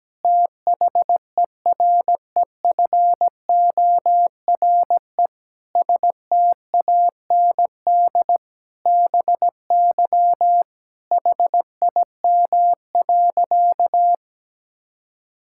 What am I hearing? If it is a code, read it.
Morse code, heard as a keyed tone, about 17 wpm, 700 Hz. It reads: THEREFORE STAND BY HIM.
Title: Sentences from Top 1000 Words - Review Part 5 - 17wpm